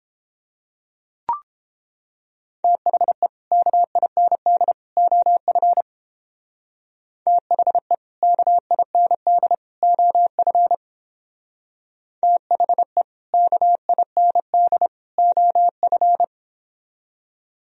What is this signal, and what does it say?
Morse code, heard as a keyed tone, about 30 wpm, 700 Hz. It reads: THE KIND OF THE KIND OF THE KIND OF